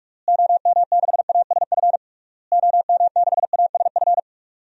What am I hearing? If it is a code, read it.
Morse code, heard as a keyed tone, about 46 wpm, 700 Hz. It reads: OM6ASF OM6ASF
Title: DXCC-100 Call Signs - 6 Characters - Repeated 3x - 45wpm